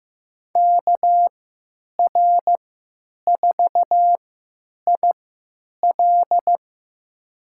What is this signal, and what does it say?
Morse code, heard as a keyed tone, about 15 wpm, spaced as for 9 wpm, 700 Hz. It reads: KR4IL
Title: USA Call Signs - 5 Characters - 3x Character Spacing - 15wpm